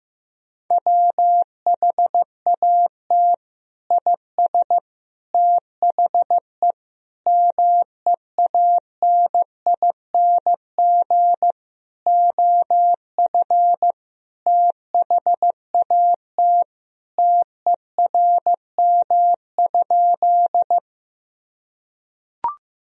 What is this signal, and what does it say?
Morse code, heard as a keyed tone, about 15 wpm, 700 Hz. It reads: WHAT IS THE MEANING OF THAT TERM?
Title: Sentences from Top 1000 Words - Review Part 4 - 15wpm